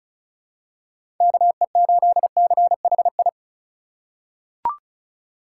Morse code, heard as a keyed tone, about 36 wpm, 700 Hz. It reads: KE8CHI
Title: USA Call Signs - 6 Characters - 2x Character Spacing - 35wpm